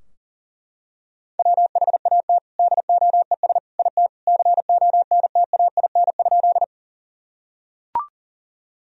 Morse code, heard as a keyed tone, about 40 wpm, 700 Hz. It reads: WHAT DOES IT CONTAIN?
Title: Sentences from Top 1000 Words - Review Part 2 - 40wpm